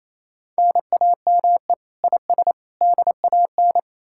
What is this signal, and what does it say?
Morse code, heard as a keyed tone, about 38 wpm, 700 Hz. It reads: NAME IS DAN